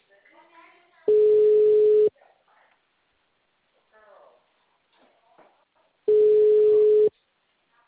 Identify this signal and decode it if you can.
Ringback tone (Europe, 425 Hz), with speech.
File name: out- -2006- - - .90793